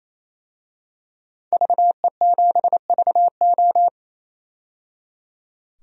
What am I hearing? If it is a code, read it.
Morse code, heard as a keyed tone, about 28 wpm, 700 Hz. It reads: VE7VO
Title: DXCC-100 Call Signs - 5 Characters - 28wpm